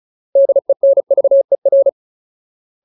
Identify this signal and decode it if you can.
Morse code, heard as a keyed tone, about 34 wpm, 550 Hz. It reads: DENVER
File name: Ultimate Word List / 35WPM Ultimate